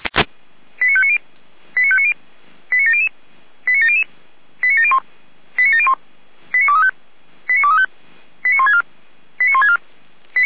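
Five-tone selective call (ZVEI), heard as quick runs of five tones, tones of about 70 ms. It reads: 89590 89590 89700 89700 89701 89701 89235 89235 89164 89164
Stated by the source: ZVEI five-tone selective call received by SDR.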